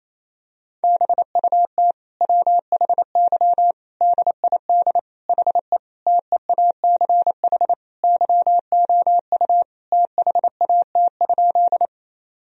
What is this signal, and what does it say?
Morse code, heard as a keyed tone, about 28 wpm, 700 Hz. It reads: BUT WHY DID HE TEACH YOU THAT?
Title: Sentences from Top 1000 Words - Review Part 2 - 28wpm